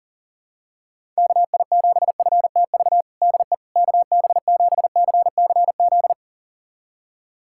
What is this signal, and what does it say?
Morse code, heard as a keyed tone, about 40 wpm, 700 Hz. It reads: KI7FTV DE KB7CCZ
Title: USA Callsign DE Callsign - 6 Characters - 40wpm